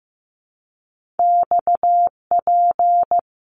Morse code, heard as a keyed tone, about 15 wpm, 700 Hz. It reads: XP
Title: Top Two Letter Combinations in Words 15wpm